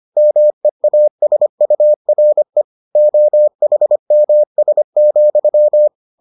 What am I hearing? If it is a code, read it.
Morse code, heard as a keyed tone, about 25 wpm, 600 Hz. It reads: MEASURE OHMS,